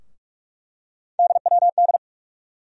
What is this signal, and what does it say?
Morse code, heard as a keyed tone, about 47 wpm, 700 Hz. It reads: DWD
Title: USA Call Sign Postfixes - 3 Letters 45wpm